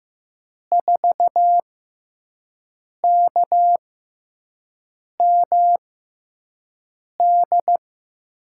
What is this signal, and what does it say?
Morse code, heard as a keyed tone, about 15 wpm, spaced as for 5 wpm, 700 Hz. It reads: 4KMD
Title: USA Call Signs - 6 Characters - 6x Character Spacing - 15wpm